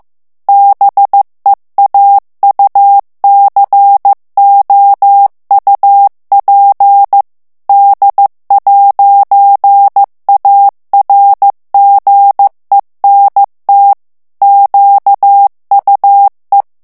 Morse code, heard as a keyed tone, about 15 wpm, 800 Hz. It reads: BEAUCOUP D'ARGENT QUE
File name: morse mp3 / FRANCAIS TEXTE CLAIR 1 VITESSE 14 MOTS MINUTES